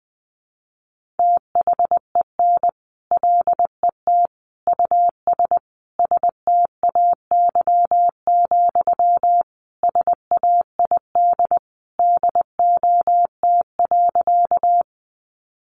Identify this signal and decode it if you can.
Morse code, heard as a keyed tone, about 20 wpm, 700 Hz. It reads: THEN LET US STAY, SAID DOT.
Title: Sentences from Top 1500 Words 20wpm